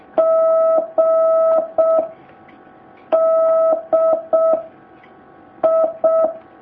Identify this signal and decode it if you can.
Morse code, heard as a keyed tone, about 6 wpm, 650 Hz. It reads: G D I
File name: jh8cbh / 2019cw